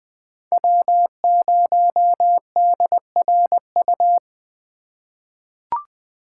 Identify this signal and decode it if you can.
Morse code, heard as a keyed tone, about 20 wpm, 700 Hz. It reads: W0DRU
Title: USA Call Signs - 4x Character Spacing - 20wpm